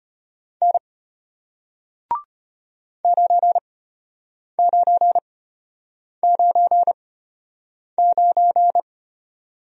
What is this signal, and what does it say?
Morse code, heard as a keyed tone, about 33 wpm, spaced as for 8 wpm, 700 Hz. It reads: N 9999